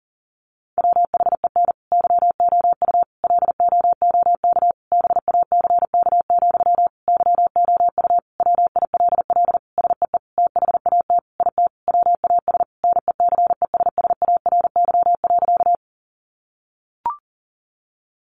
Morse code, heard as a keyed tone, about 40 wpm, 700 Hz. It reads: WHEN YOU LOOK BACK, YOU WILL SEE THAT IT WAS NECESSARY.
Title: Sentences from Top 800 Words 40wpm